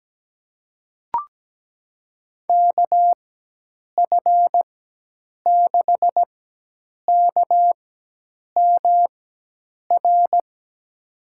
Morse code, heard as a keyed tone, about 17 wpm, spaced as for 8 wpm, 700 Hz. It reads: KF6KMR
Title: USA Call Signs - 6 Characters - 4x Character Spacing - 17wpm